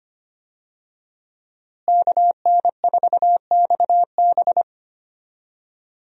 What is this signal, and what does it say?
Morse code, heard as a keyed tone, about 25 wpm, 700 Hz. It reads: KN4XB
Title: USA Call Signs - 5 Characters - Repeated 3x - 25wpm